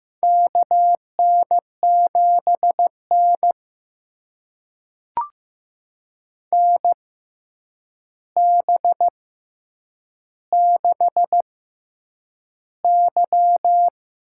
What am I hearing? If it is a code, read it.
Morse code, heard as a keyed tone, about 15 wpm, spaced as for 8 wpm, 700 Hz. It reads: KN7N NB6Y
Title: USA Call Signs - 6x Character Spacing - 15wpm